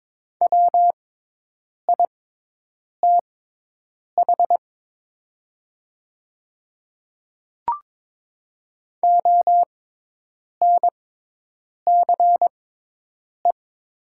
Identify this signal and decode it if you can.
Morse code, heard as a keyed tone, about 22 wpm, spaced as for 8 wpm, 700 Hz. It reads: WITH ONCE
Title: 6x Character Spacing - Top 300 Words - 22wpm